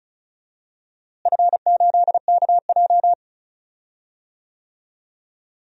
Morse code, heard as a keyed tone, about 36 wpm, 700 Hz. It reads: F8KJ
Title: DXCC-100 Call Signs - 4 Characters - Repeated 3x - 35wpm